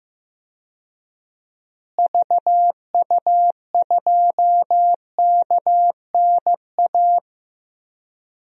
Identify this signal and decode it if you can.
Morse code, heard as a keyed tone, about 15 wpm, 700 Hz. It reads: VU2KNA
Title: DXCC-100 Call Signs - 6 Characters - Repeated 2x - 15wpm